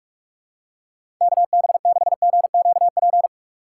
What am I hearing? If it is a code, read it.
Morse code, heard as a keyed tone, about 43 wpm, 700 Hz. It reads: KB6GQP
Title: USA Call Signs - 6 Characters - 4x Character Spacing - 45wpm